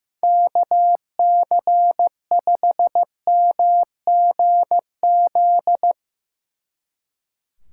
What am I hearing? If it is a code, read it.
Morse code, heard as a keyed tone, about 16 wpm, 700 Hz. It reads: KC5MGZ
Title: Speed Racing - USA 2x3 Call Signs 15wpm